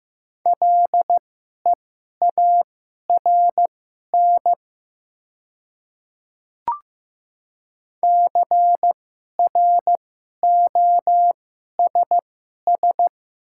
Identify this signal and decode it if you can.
Morse code, heard as a keyed tone, about 15 wpm, spaced as for 11 wpm, 700 Hz. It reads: LEARN CROSS
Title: 2x Character Spacing - Top 300 Words - 15wpm